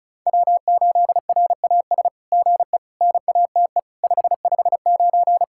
Morse code, heard as a keyed tone, about 33 wpm, 700 Hz. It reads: W8RAS GE NATE 559
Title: SOTA QSO Practice 35wpm